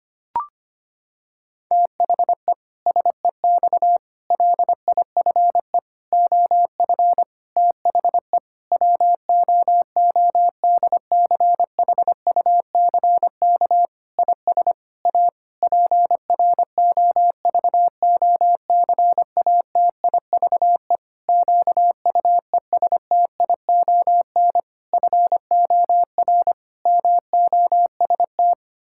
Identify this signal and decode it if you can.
Morse code, heard as a keyed tone, about 25 wpm, 700 Hz. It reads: THE SEX LIFE OF THE WOODCHUCK IS A PROVOCATIVE QUESTION FOR MOST
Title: Pangram Challenge v2 - 25wpm